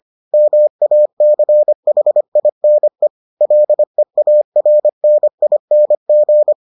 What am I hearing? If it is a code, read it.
Morse code, heard as a keyed tone, about 25 wpm, 600 Hz. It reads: MACHINE LEARNING